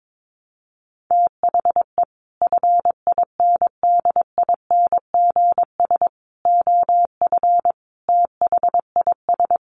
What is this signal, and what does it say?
Morse code, heard as a keyed tone, about 22 wpm, 700 Hz. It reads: THE FINDINGS OF THIS
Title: Sets of 5 Words Encore2 22wpm